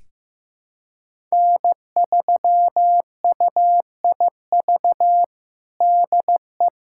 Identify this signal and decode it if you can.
Morse code, heard as a keyed tone, about 15 wpm, 700 Hz. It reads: N3UIV DE
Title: USA Callsign DE Callsign - 5 Characters - 15wpm